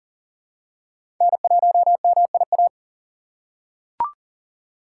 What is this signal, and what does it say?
Morse code, heard as a keyed tone, about 40 wpm, 700 Hz. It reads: N1MIA
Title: USA Call Signs - 6x Character Spacing - 40wpm